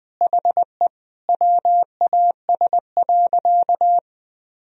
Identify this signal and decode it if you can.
Morse code, heard as a keyed tone, about 21 wpm, 700 Hz. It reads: HE WAS.
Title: Sentences from Top 1000 Words - Review Part 1 - 20wpm